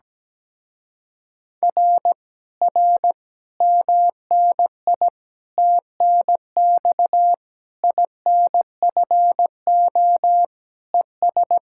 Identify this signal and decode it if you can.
Morse code, heard as a keyed tone, about 17 wpm, 700 Hz. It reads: R R MNI TNX INFO ES
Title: Speed-Racing Common QSO Phrases - 17wpm